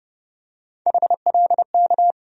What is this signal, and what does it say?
Morse code, heard as a keyed tone, about 30 wpm, 700 Hz. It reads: HLK
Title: USA Call Sign Postfixes - 3 Letters 30wpm